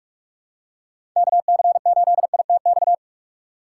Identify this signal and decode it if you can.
Morse code, heard as a keyed tone, about 41 wpm, 700 Hz. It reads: KC8ITX
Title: USA Call Signs - 6 Characters - 2x Character Spacing - 45wpm